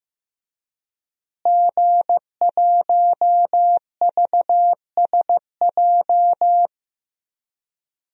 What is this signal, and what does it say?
Morse code, heard as a keyed tone, about 15 wpm, 700 Hz. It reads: G1VSJ
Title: DXCC-100 Call Signs - 5 Characters - Repeated 2x - 15wpm